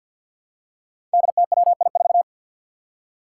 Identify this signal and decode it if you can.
Morse code, heard as a keyed tone, about 46 wpm, 700 Hz. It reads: DTWI4
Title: Five Challenging Letter-Number Combinations 50wpm